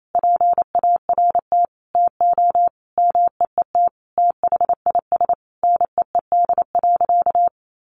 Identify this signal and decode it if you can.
Morse code, heard as a keyed tone, about 28 wpm, 700 Hz. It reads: PART TO MEET THIS NEED.
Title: Sentences from Top 1100 Words 28wpm